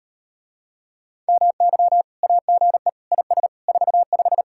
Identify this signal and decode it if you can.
Morse code, heard as a keyed tone, about 38 wpm, 700 Hz. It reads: MY AGE IS 45